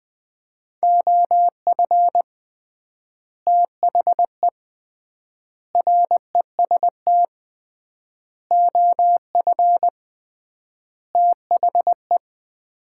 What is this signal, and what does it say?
Morse code, heard as a keyed tone, about 20 wpm, 700 Hz. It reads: OF THE REST OF THE
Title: 3x Word-Spacing - Sets of 5 Words - 20wpm